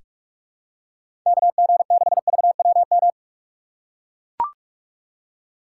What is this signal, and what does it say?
Morse code, heard as a keyed tone, about 42 wpm, 700 Hz. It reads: KG6VWM
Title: USA Call Signs - 6 Characters - 6x Character Spacing - 45wpm